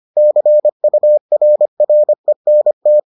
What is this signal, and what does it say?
Morse code, heard as a keyed tone, about 25 wpm, 600 Hz. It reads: CURRENT